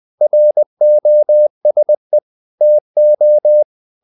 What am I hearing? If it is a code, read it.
Morse code, heard as a keyed tone, about 20 wpm, 600 Hz. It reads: ROSE TO
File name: the old man and the sea part 2 20wpm